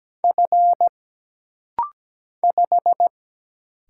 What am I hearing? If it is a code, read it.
Morse code, heard as a keyed tone, about 17 wpm, 700 Hz. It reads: F 5
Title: Single Letter-Number Alt - Rapid-Fire 17wpm